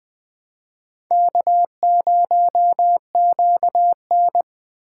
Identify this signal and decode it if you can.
Morse code, heard as a keyed tone, about 20 wpm, 700 Hz. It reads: K0QN
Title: USA Call Signs - 4 Characters - 4x Character Spacing - 20wpm